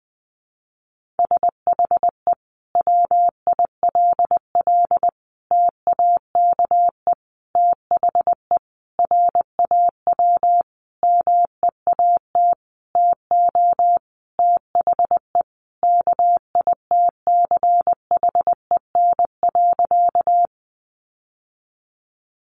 Morse code, heard as a keyed tone, about 20 wpm, 700 Hz. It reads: SHE WILL TAKE THE RAW MEAT TO THE KITCHEN.
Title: Sentences from Top 1500 Words 20wpm